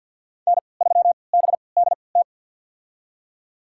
Morse code, heard as a keyed tone, about 48 wpm, spaced as for 28 wpm, 700 Hz. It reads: N3BDT
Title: USA Call Signs - 3x Character Spacing - 50wpm